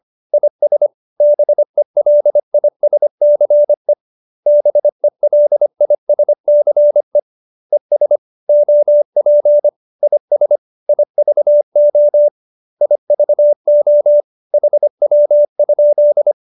Morse code, heard as a keyed tone, about 25 wpm, 590 Hz. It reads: IS BELISCE BELISCE ES OP IS IVO IVO HW?